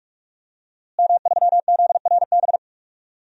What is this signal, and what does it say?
Morse code, heard as a keyed tone, about 46 wpm, 700 Hz. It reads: M3ZRB
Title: DXCC-100 Call Signs - 5 Characters - Repeated 3x - 45wpm